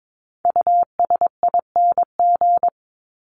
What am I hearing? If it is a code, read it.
Morse code, heard as a keyed tone, about 22 wpm, 700 Hz. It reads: USING
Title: ING Suffix - Top 100 Words with Base Words 22wpm